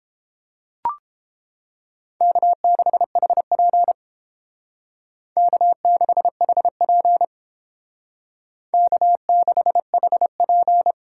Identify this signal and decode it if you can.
Morse code, heard as a keyed tone, about 30 wpm, 700 Hz. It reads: K6HP K6HP K6HP